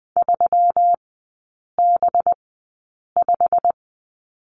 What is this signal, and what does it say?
Morse code, heard as a keyed tone, about 20 wpm, spaced as for 8 wpm, 700 Hz. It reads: 3B5
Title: 184 Lesson 30e - Letter b - Three Characters 20wpm